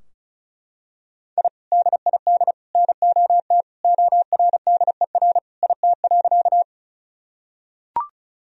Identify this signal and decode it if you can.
Morse code, heard as a keyed tone, about 34 wpm, 700 Hz. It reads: I DID NOT ORDER IT.
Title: Sentences from Top 1000 Words - Review Part 2 - 35wpm